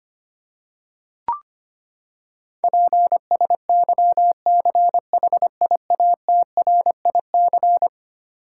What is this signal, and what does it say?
Morse code, heard as a keyed tone, about 25 wpm, 700 Hz. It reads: PSYCHIATRIC
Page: morse-code-ninja-practice-25wpm